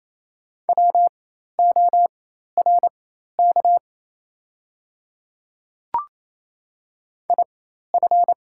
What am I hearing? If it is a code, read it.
Morse code, heard as a keyed tone, about 27 wpm, spaced as for 13 wpm, 700 Hz. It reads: WORK IF
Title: 4x Character Spacing - Top 200 Words - 28wpm